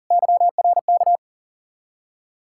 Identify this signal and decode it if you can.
Morse code, heard as a keyed tone, about 40 wpm, 700 Hz. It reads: YRK
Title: Ontario QSO Party Exchange 40wpm